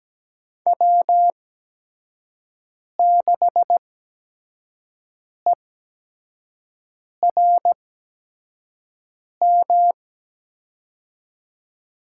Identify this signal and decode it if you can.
Morse code, heard as a keyed tone, about 17 wpm, spaced as for 5 wpm, 700 Hz. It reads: W6ERM
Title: USA Call Signs - 5 Characters - 8x Character Spacing - 17wpm